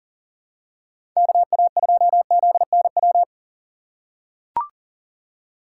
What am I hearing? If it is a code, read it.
Morse code, heard as a keyed tone, about 40 wpm, 700 Hz. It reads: KA2ZNW
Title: USA Call Signs - 6 Characters - 3x Character Spacing - 40wpm